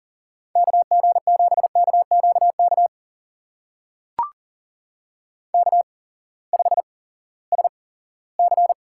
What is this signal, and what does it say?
Morse code, heard as a keyed tone, about 40 wpm, spaced as for 20 wpm, 700 Hz. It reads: KG7KQK K5SC
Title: USA Call Signs - 8x Character Spacing - 40wpm